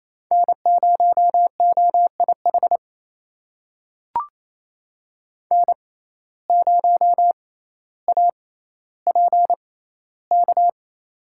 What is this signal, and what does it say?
Morse code, heard as a keyed tone, about 28 wpm, spaced as for 14 wpm, 700 Hz. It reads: N0OIH N0APK
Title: USA Call Signs - 5 Characters - 6x Character Spacing - 28wpm